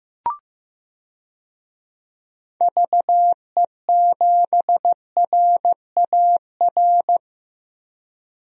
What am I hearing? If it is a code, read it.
Morse code, heard as a keyed tone, about 15 wpm, 700 Hz. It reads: VE7RAR
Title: DXCC-100 Call Signs - 6 Characters - 15wpm